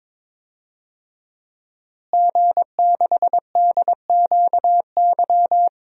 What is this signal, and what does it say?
Morse code, heard as a keyed tone, about 22 wpm, 700 Hz. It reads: G6DQY
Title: DXCC-100 Call Signs - 5 Characters - Repeated 2x - 22wpm